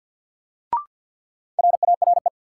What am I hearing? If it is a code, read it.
Morse code, heard as a keyed tone, about 50 wpm, 700 Hz. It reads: RARE
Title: Sentences from Top 1100 Words - New Word Drill - 50wpm